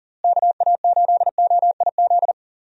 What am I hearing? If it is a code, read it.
Morse code, heard as a keyed tone, about 39 wpm, 700 Hz. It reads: KA8OIZ
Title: USA Call Signs - 6 Characters - Repeated 3x - 40wpm